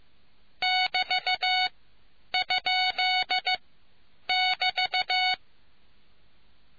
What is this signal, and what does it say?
Morse code, heard as a keyed tone, about 15 wpm, spaced as for 9 wpm, 740 Hz. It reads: <BT>?<BT>